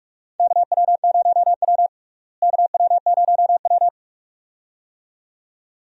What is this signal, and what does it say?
Morse code, heard as a keyed tone, about 43 wpm, 700 Hz. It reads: KW0W KW0W
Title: USA Call Signs - 4 Characters - Repeated 2x - 45wpm